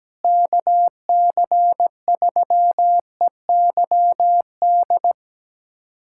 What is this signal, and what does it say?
Morse code, heard as a keyed tone, about 17 wpm, 700 Hz. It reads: KC3EYD
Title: USA Callsign DE Callsign 17wpm